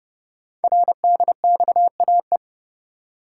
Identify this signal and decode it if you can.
Morse code, heard as a keyed tone, about 30 wpm, 700 Hz. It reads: RDXAE
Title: Code Groups - 5 Letters - 30wpm